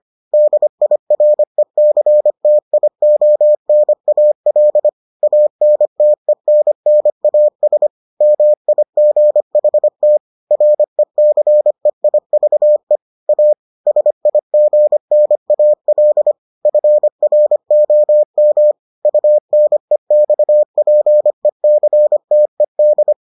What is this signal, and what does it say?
Morse code, heard as a keyed tone, about 25 wpm, 600 Hz. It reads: DIRECTIONAL ANTENNAS MIGHT RECEIVE A SIGNAL FROM UNEXPECTED